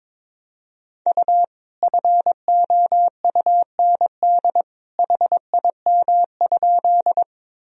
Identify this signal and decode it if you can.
Morse code, heard as a keyed tone, about 22 wpm, 700 Hz. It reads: U FOUND HIM?
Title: Sentences from Top 300 Words and Abbreviations 22wpm